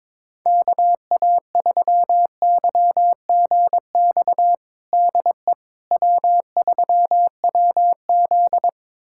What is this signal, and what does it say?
Morse code, heard as a keyed tone, about 22 wpm, 700 Hz. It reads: KA3YGX DE W3WZ
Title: USA Callsign DE Callsign 22wpm